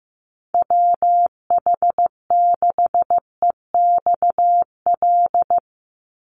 Morse code, heard as a keyed tone, about 15 wpm, 700 Hz. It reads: WH6EXL
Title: USA Call Signs 15wpm